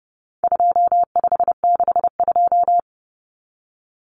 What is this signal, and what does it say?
Morse code, heard as a keyed tone, about 30 wpm, 700 Hz. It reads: 2562
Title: Common Cut Numbers - 4-Digit 30wpm